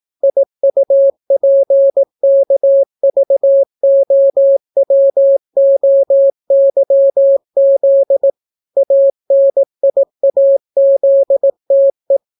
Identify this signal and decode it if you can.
Morse code, heard as a keyed tone, about 18 wpm, 550 Hz. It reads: IUPKVOWOYZ ANIAZTE